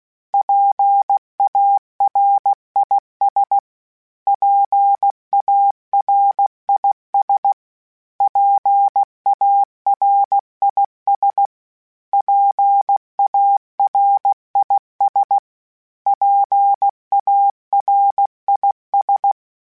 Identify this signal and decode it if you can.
Morse code, heard as a keyed tone, about 16 wpm, 800 Hz. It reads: PARIS PARIS PARIS PARIS PARIS
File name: PARIS Calibration / PARIS mp3s / PARIS@15